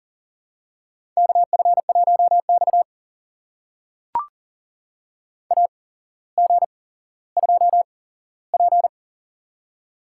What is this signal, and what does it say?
Morse code, heard as a keyed tone, about 39 wpm, spaced as for 17 wpm, 700 Hz. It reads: KF1X AG2P
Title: USA Call Signs - 8x Character Spacing - 40wpm